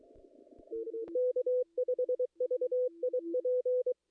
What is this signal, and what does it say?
Morse code, heard as a keyed tone, about 23 wpm, 500 Hz. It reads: K5VIP
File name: contests / ss-cw-k5tr-op-n5rz -2300